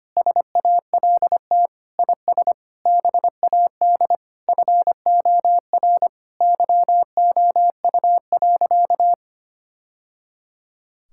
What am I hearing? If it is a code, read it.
Morse code, heard as a keyed tone, about 25 wpm, 700 Hz. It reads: SALT IS BAD FOR YOU.
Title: Sentences from Top 1000 Words - Review Part 5 - 25wpm